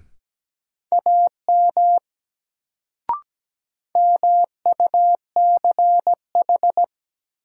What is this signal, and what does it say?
Morse code, heard as a keyed tone, about 17 wpm, 700 Hz. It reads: AM MUCH
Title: Sets of 2 Words - New Word Drill - 17wpm